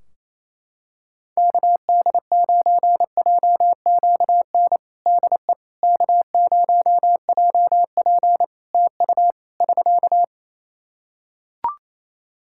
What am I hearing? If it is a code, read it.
Morse code, heard as a keyed tone, about 28 wpm, 700 Hz. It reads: KD9JQN DE K0JP TU <SK>
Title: Practice QSO - Exchange 4 - Ending - 28wpm